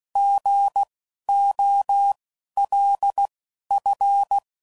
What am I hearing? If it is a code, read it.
Morse code, heard as a keyed tone, about 16 wpm, spaced as for 11 wpm, 800 Hz. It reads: GOLF